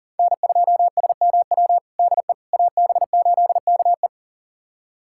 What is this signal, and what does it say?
Morse code, heard as a keyed tone, about 40 wpm, 700 Hz. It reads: N2SMW DE AB8KE
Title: USA Callsign DE Callsign - 5 Characters - 40wpm